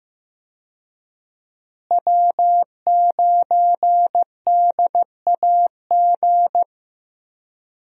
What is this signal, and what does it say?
Morse code, heard as a keyed tone, about 15 wpm, 700 Hz. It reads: W9DAG
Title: USA Call Signs - 5 Characters - Repeated 3x - 15wpm